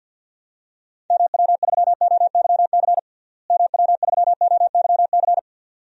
Morse code, heard as a keyed tone, about 50 wpm, 700 Hz. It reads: MW3OYC MW3OYC
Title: DXCC-100 Call Signs - 6 Characters - Repeated 2x - 50wpm